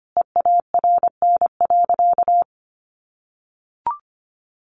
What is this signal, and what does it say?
Morse code, heard as a keyed tone, about 25 wpm, 700 Hz. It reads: EARN.
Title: Sentences from Top 1500 Words 25wpm